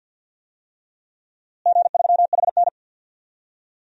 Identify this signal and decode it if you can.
Morse code, heard as a keyed tone, about 53 wpm, 700 Hz. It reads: G3HN
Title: DXCC-100 Call Signs - 4 Characters - Repeated 3x - 50wpm